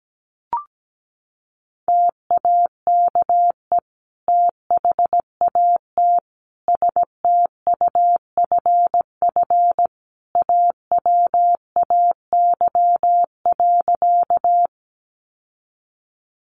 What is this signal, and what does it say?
Morse code, heard as a keyed tone, about 17 wpm, 700 Hz. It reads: TAKE THAT STUFF AWAY.
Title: Sentences from Top 1400 Words 17wpm